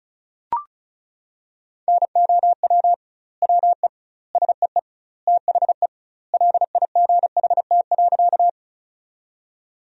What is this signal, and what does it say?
Morse code, heard as a keyed tone, about 36 wpm, 700 Hz. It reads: NOW WE SEE THE LIGHT.
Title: 2x Word-Spacing - Sentences from Top 200 Words - 35wpm